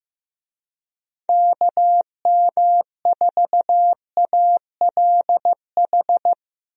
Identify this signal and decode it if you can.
Morse code, heard as a keyed tone, about 15 wpm, 700 Hz. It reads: KM4ALH
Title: USA Call Signs - 6 Characters - 15wpm